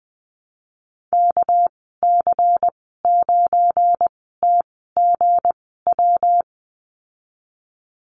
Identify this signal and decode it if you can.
Morse code, heard as a keyed tone, about 20 wpm, spaced as for 15 wpm, 700 Hz. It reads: KC9TGW
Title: 207 Lesson 33g - Letter j - Call Signs 20wpm